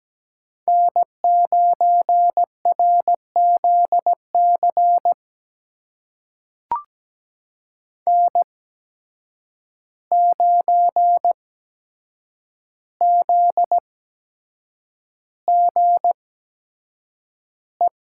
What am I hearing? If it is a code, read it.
Morse code, heard as a keyed tone, about 17 wpm, spaced as for 7 wpm, 700 Hz. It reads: N9RZC N9ZGE
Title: USA Call Signs - 8x Character Spacing - 17wpm